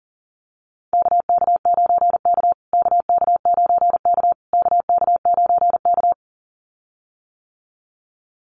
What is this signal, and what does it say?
Morse code, heard as a keyed tone, about 40 wpm, 700 Hz. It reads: KK9K KK9K KK9K
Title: USA Call Signs Repeated 3x 40wpm